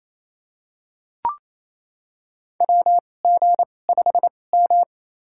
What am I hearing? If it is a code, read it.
Morse code, heard as a keyed tone, about 28 wpm, spaced as for 20 wpm, 700 Hz. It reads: WG5M
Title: USA Call Signs - 2x Character Spacing - 28wpm